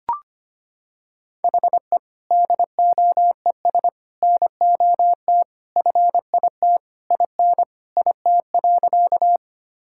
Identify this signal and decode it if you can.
Morse code, heard as a keyed tone, about 25 wpm, 700 Hz. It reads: HE DOES NOT FIT IN IT.
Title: Sentences from Top 1000 Words - Review Part 5 - 25wpm